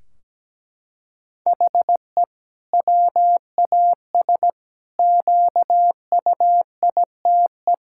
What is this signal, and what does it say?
Morse code, heard as a keyed tone, about 17 wpm, 700 Hz. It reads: HE WAS QUITE
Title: Sentences from Top 1600 Words 17wpm